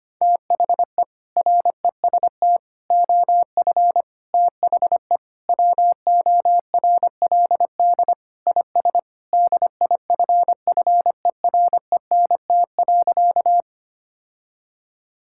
Morse code, heard as a keyed tone, about 25 wpm, 700 Hz. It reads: THE REST OF THE WORLD IS DIFFERENT.
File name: practice / 25wpm / Sentences with 4-Word Phrases - 25wpm